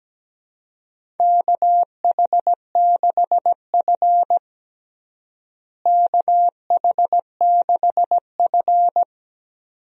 Morse code, heard as a keyed tone, about 17 wpm, 700 Hz. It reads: KH6F KH6F
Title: DXCC-100 Call Signs - 4 Characters - Repeated 2x - 17wpm